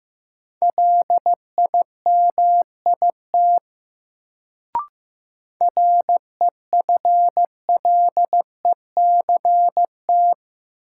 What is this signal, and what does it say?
Morse code, heard as a keyed tone, about 15 wpm, 700 Hz. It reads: LIMIT REFLECT
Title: Sentences from Top 1500 Words - New Word Drill Alt - 15wpm